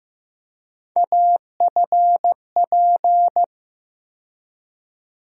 Morse code, heard as a keyed tone, about 15 wpm, 700 Hz. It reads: AFP
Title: USA Call Sign Postfixes - 3 Letters 15wpm